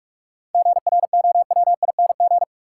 Morse code, heard as a keyed tone, about 46 wpm, 700 Hz. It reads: GROWING
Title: Sets of 5 Words Encore - New Word Drill - 45wpm